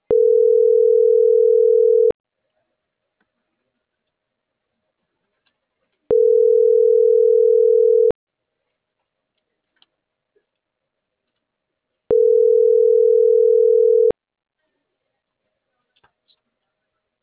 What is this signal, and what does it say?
Ringback tone (North America), 440 + 480 Hz.